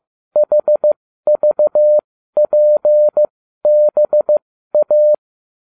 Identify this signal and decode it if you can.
Morse code, heard as a keyed tone, about 15 wpm, spaced as for 12 wpm, 600 Hz. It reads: HVPBA